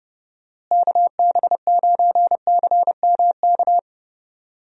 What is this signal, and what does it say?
Morse code, heard as a keyed tone, about 32 wpm, 700 Hz. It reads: KB9CMK
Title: USA Call Signs - 6 Characters - Repeated 2x - 30wpm